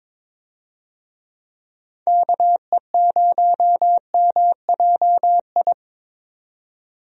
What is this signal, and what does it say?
Morse code, heard as a keyed tone, about 22 wpm, 700 Hz. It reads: KE0MJI
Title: USA Call Signs - 6 Characters - 22wpm